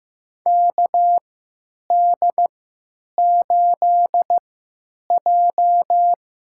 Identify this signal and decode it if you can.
Morse code, heard as a keyed tone, about 15 wpm, spaced as for 9 wpm, 700 Hz. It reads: KD8J
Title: USA Call Signs - 6 Characters - 3x Character Spacing - 15wpm